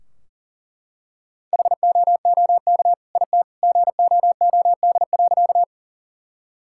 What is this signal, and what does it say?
Morse code, heard as a keyed tone, about 38 wpm, 700 Hz. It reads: HOOK IT GOOD.
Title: Sentences from Top 1600 Words 40wpm